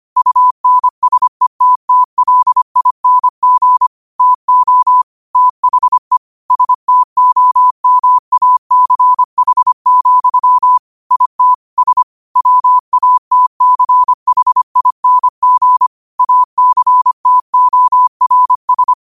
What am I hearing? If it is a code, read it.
Morse code, heard as a keyed tone, about 25 wpm, 1000 Hz. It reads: UNSETTLING TO THE STOMACH, IT S WATCHING ACTORS
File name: qotdfiles / 25wpm